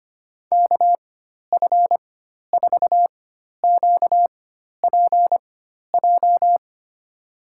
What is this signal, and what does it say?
Morse code, heard as a keyed tone, about 25 wpm, spaced as for 12 wpm, 700 Hz. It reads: KF4QPJ
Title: USA Call Signs - 6 Characters - 4x Character Spacing - 25wpm